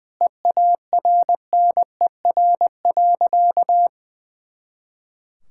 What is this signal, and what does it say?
Morse code, heard as a keyed tone, about 20 wpm, 700 Hz. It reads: EARNER.
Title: Pangram Challenge v2 - 20wpm